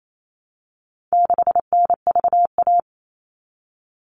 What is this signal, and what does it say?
Morse code, heard as a keyed tone, about 28 wpm, 700 Hz. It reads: 6NVA
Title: All Cut Numbers - 4-Digit 28wpm